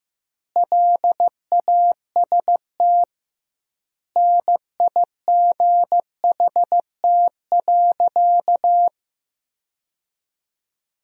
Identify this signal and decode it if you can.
Morse code, heard as a keyed tone, about 15 wpm, 700 Hz. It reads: LAST NIGHT.
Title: 2x Word-Spacing - Sentences from Top 300 Words - 15wpm